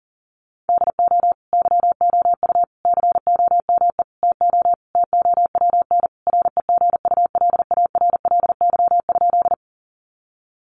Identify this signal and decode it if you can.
Morse code, heard as a keyed tone, about 40 wpm, 700 Hz. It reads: DO YOU COME TO TOWN REGULARLY?
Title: Sentences from Top 1400 Words 40wpm